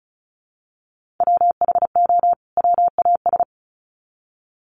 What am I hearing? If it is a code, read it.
Morse code, heard as a keyed tone, about 34 wpm, 700 Hz. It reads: WHO WAS
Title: Sets of 2 Words 35wpm